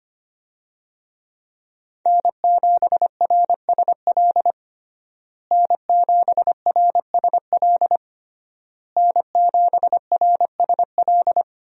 Morse code, heard as a keyed tone, about 25 wpm, 700 Hz. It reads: N7RSL N7RSL N7RSL
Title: USA Call Signs - 5 Characters - Repeated 3x - 25wpm